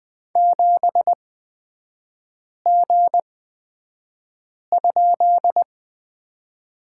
Morse code, heard as a keyed tone, about 20 wpm, spaced as for 5 wpm, 700 Hz. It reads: 7G?
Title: 255 Lesson 41e - Period - Three Characters 20wpm